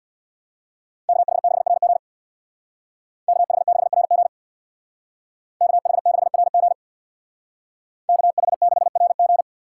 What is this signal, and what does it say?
Morse code, heard as a keyed tone, about 58 wpm, 700 Hz. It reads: KH6RG KH6RG KH6RG KH6RG
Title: Speed Racing - USA Call Signs 50wpm